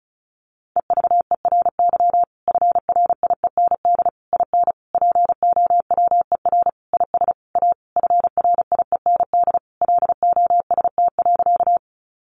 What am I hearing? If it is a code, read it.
Morse code, heard as a keyed tone, about 34 wpm, 700 Hz. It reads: EVERY FRIEND IN POWER IS A FRIEND LOST.
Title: Sentences from Top 1400 Words 35wpm